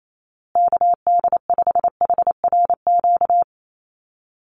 Morse code, heard as a keyed tone, about 28 wpm, 700 Hz. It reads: KD5HRQ
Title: USA Call Signs Repeated 3x 28wpm